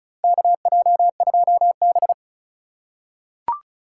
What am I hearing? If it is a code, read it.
Morse code, heard as a keyed tone, about 36 wpm, 700 Hz. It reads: KJ2B
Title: USA Call Signs - 4 Characters - 3x Character Spacing - 35wpm